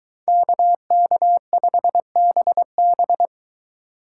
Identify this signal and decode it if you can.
Morse code, heard as a keyed tone, about 23 wpm, 700 Hz. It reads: KK5BB